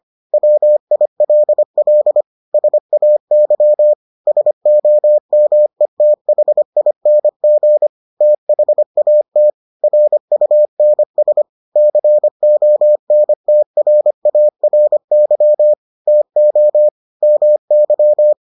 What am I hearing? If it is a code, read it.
Morse code, heard as a keyed tone, about 25 wpm, 600 Hz. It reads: WILL SAY SOMETHING THAT RUNS CONTRARY TO MY